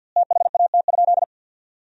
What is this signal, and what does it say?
Morse code, heard as a keyed tone, about 50 wpm, 700 Hz. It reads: THAT?
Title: 4x Word-Spacing - Sentences from Top 200 Words - 50wpm